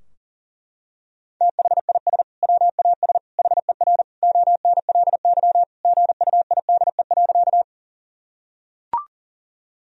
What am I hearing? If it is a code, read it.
Morse code, heard as a keyed tone, about 40 wpm, 700 Hz. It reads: THIS WAS HER ONLY GUIDE.
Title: Sentences from Top 1000 Words - Review Part 3 - 40wpm